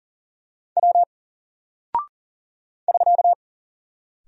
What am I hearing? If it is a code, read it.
Morse code, heard as a keyed tone, about 40 wpm, 700 Hz. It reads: W <SK>
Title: Single Character (Letter:Number:Punctuation:Prosign) 40wpm